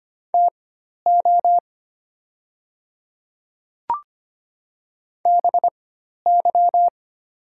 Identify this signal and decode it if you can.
Morse code, heard as a keyed tone, about 25 wpm, spaced as for 12 wpm, 700 Hz. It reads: TO BY
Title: 4x Character Spacing - Top 100 Words - 25wpm